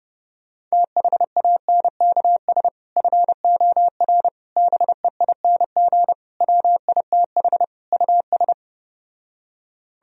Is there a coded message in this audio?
Yes: Morse code from 0.7 to 8.5 s, heard as a keyed tone, about 33 wpm, 700 Hz. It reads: THANKS FOR BEING WITH US